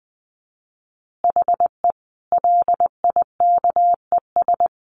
Morse code, heard as a keyed tone, about 19 wpm, 700 Hz. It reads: HE LIKES
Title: Additional Easy Phrases 20wpm